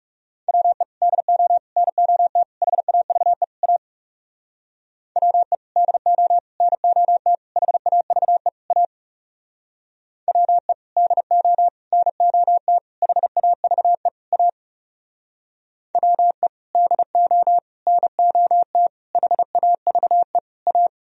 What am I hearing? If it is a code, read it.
Morse code, heard as a keyed tone, about 37 wpm, 700 Hz. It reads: WE DO NOT HAVE A WE DO NOT HAVE A WE DO NOT HAVE A WE DO NOT HAVE A
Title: Speed-Racing - Sets of 5 Words 30wpm